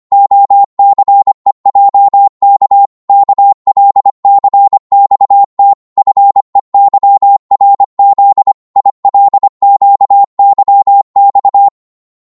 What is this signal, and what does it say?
Morse code, heard as a keyed tone, about 25 wpm, 800 Hz. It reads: OCEJK KLCXT FEYRZ ILQYX
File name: Morse / ltr-25wpm